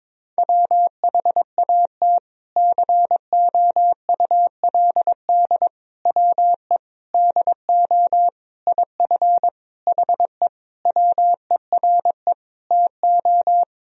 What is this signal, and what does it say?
Morse code, heard as a keyed tone, about 22 wpm, 700 Hz. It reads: WHAT COULD WE DO IF HE WERE TO
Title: Sentences from Top 400 Words and Abbreviations 22wpm